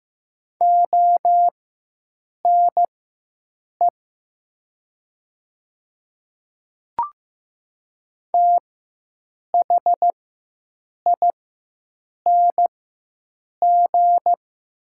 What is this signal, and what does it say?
Morse code, heard as a keyed tone, about 15 wpm, spaced as for 7 wpm, 700 Hz. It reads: ONE THING
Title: 4x Character Spacing - Top 100 Words - 15wpm